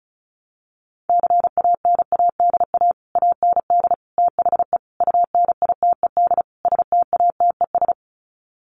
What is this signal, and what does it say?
Morse code, heard as a keyed tone, about 34 wpm, 700 Hz. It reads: CANADA AND THE UNITED STATES
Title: Sets of 5 Words Encore 35wpm